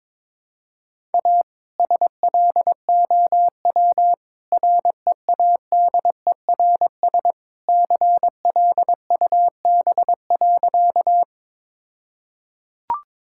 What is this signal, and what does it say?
Morse code, heard as a keyed tone, about 22 wpm, 690 Hz. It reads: A SLOW READERS CLUB.
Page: morse-code-ninja-practice-22wpm